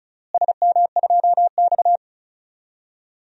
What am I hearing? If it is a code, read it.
Morse code, heard as a keyed tone, about 34 wpm, 700 Hz. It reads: SM2X
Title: DXCC-100 Call Signs - 4 Characters - Repeated 3x - 35wpm